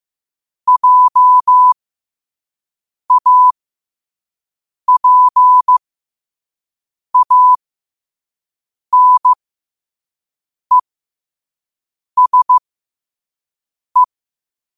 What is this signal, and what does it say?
Morse code, heard as a keyed tone, about 15 wpm, spaced as for 5 wpm, 1000 Hz. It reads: JAPANESE
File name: qotdfiles / 5wpm